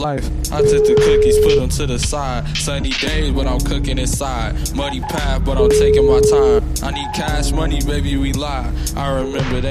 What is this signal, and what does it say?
Ringback tone (Europe, 425 Hz), with speech.